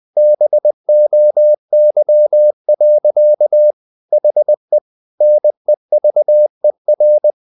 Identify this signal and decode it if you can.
Morse code, heard as a keyed tone, about 20 wpm, 600 Hz. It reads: BOY. HE NEVER
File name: the old man and the sea / the old man and the sea part 1 20wpm